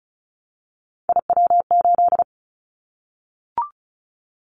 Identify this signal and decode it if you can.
Morse code, heard as a keyed tone, about 34 wpm, 700 Hz. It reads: IW8
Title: Callsign Prefixes for Top 50 DXCC - 35wpm